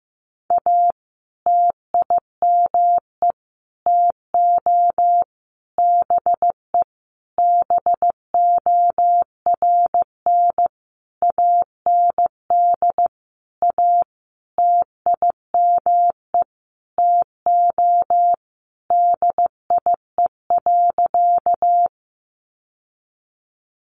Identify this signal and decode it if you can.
Morse code, heard as a keyed tone, about 15 wpm, 700 Hz. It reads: A TIME TO BE BORN AND A TIME TO DIE.
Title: Sentences from Top 1300 Words 15wpm